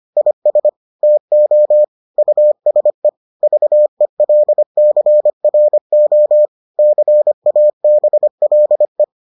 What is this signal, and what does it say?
Morse code, heard as a keyed tone, about 25 wpm, 600 Hz. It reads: IS TO USE VELCRO CABLE